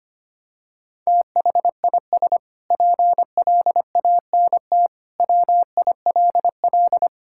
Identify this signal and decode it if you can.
Morse code, heard as a keyed tone, about 25 wpm, 700 Hz. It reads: THIS PLANT WILL
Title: Sentences from Top 1000 Words - Review Part 6 - 25wpm